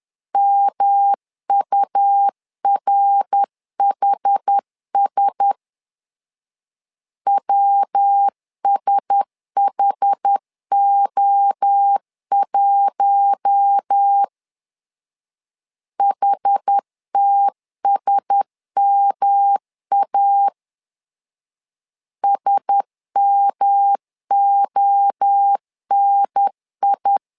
Morse code, heard as a keyed tone, about 10 wpm, 800 Hz. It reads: MURHS WSHO1 HTSMA SMONI
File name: LEZIONI / 4 Q1S 10wpm 2133